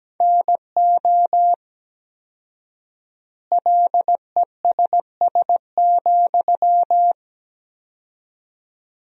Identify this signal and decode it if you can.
Morse code, heard as a keyed tone, about 17 wpm, 700 Hz. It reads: NO LESS,
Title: 4x Word-Spacing - Sentences from Top 400 Words - 17wpm